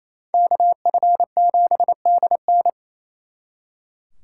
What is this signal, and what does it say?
Morse code, heard as a keyed tone, about 30 wpm, 700 Hz. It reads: KF7DN
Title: Speed Racing - USA 2x2 Call Signs 28wpm